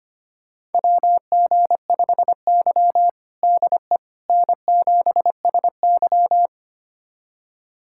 Morse code, heard as a keyed tone, about 25 wpm, 700 Hz. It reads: WG5Y DE N7SY
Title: USA Callsign DE Callsign - 4 Characters - 25wpm